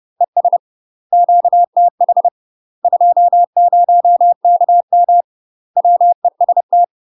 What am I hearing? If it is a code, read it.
Morse code, heard as a keyed tone, about 30 wpm, 700 Hz. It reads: ES QTH 20KM WEST
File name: UK MASTER QSOs TRAINING / 30WPM / UK MASTER QSOs 0024